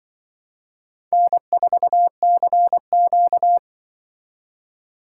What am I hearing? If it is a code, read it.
Morse code, heard as a keyed tone, about 25 wpm, 700 Hz. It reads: N4CQ
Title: Speed Racing - USA Call Signs 20wpm